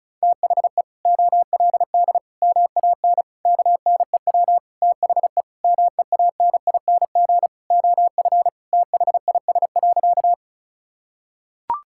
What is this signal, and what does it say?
Morse code, heard as a keyed tone, about 34 wpm, 700 Hz. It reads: THE OLD MAN KNEW THE MEANING OF THIS.
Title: Sentences from Top 1000 Words - Review Part 6 - 35wpm